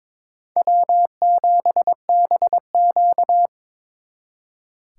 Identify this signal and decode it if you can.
Morse code, heard as a keyed tone, about 26 wpm, 700 Hz. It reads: W7BQ
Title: Speed Racing - USA 1x2 Call Signs 22wpm